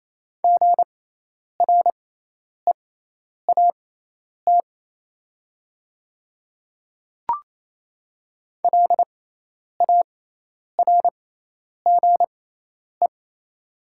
Morse code, heard as a keyed tone, about 28 wpm, spaced as for 10 wpm, 700 Hz. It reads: GREAT LARGE
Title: 6x Character Spacing - Top 200 Words - 28wpm